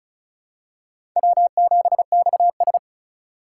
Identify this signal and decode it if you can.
Morse code, heard as a keyed tone, about 34 wpm, 700 Hz. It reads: W7XS
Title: USA Call Signs - 4 Characters - 2x Character Spacing - 35wpm